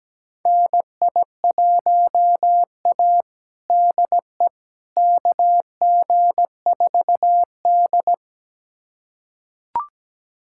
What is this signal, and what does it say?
Morse code, heard as a keyed tone, about 17 wpm, 700 Hz. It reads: NI1A DE KG4D
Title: USA Callsign DE Callsign - 4 Characters - 17wpm